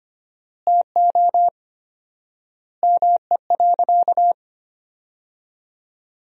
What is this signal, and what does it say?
Morse code, heard as a keyed tone, about 25 wpm, 700 Hz. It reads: TO ME.
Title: 4x Word-Spacing - Sentences from Top 200 Words - 25wpm